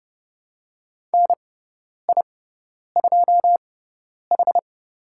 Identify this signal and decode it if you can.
Morse code, heard as a keyed tone, about 30 wpm, spaced as for 10 wpm, 700 Hz. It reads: NI2H
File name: Lesson 16g - Letter c - Call Signs 30wpm